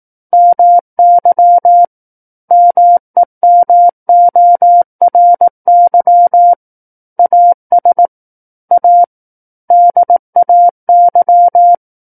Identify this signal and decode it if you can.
Morse code, heard as a keyed tone, about 18 wpm, 700 Hz. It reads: MY MEMORY AS A DAY
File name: 18 WPM